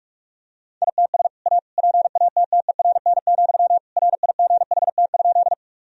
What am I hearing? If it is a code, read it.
Morse code, heard as a keyed tone, about 41 wpm, 700 Hz. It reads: ITS A PATTERN, RIGHT?
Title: Sentences from Top 1000 Words - Review Part 2 - 45wpm